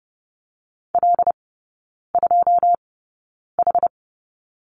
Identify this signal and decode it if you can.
Morse code, heard as a keyed tone, about 30 wpm, spaced as for 9 wpm, 700 Hz. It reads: L2H
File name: Lesson 19e - Letter w - Three Characters 30wpm